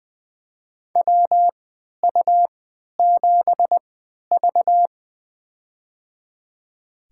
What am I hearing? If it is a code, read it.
Morse code, heard as a keyed tone, about 21 wpm, spaced as for 12 wpm, 700 Hz. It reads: WU7V
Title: USA Call Signs - 4 Characters - 3x Character Spacing - 20wpm